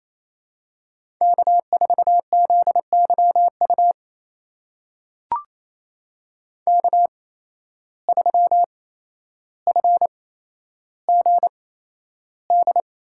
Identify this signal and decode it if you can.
Morse code, heard as a keyed tone, about 28 wpm, spaced as for 12 wpm, 700 Hz. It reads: K4ZYU K3FGD
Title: USA Call Signs - 5 Characters - 8x Character Spacing - 28wpm